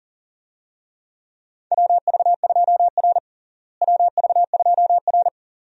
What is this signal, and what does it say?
Morse code, heard as a keyed tone, about 40 wpm, 700 Hz. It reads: WV2R WV2R
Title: USA Call Signs - 4 Characters - Repeated 2x - 40wpm